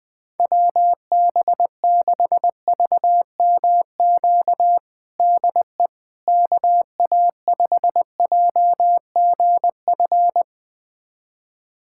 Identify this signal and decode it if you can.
Morse code, heard as a keyed tone, about 21 wpm, 700 Hz. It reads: WB6VMQ DE KA5JGF
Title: USA Callsign DE Callsign - 6 Characters - 20wpm